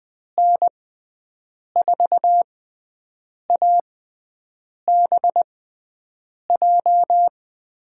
Morse code, heard as a keyed tone, about 21 wpm, spaced as for 7 wpm, 700 Hz. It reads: N4ABJ
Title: USA Call Signs - 6 Characters - 6x Character Spacing - 20wpm